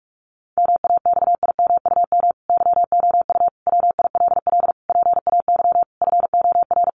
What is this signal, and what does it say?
Morse code, heard as a keyed tone, about 44 wpm, 700 Hz. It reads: MAXIMUM YOU WILL PAY FOR
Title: Sentences from Top 1300 Words 45wpm